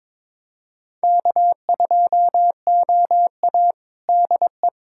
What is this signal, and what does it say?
Morse code, heard as a keyed tone, about 22 wpm, 700 Hz. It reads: K2OA DE
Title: USA Callsign DE Callsign - 4 Characters - 22wpm